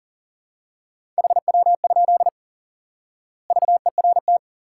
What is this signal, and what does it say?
Morse code, heard as a keyed tone, about 40 wpm, 700 Hz. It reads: HW? VERT
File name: IQE Course 3 - Lesson 89 - YAGI - Two Elements - 40wpm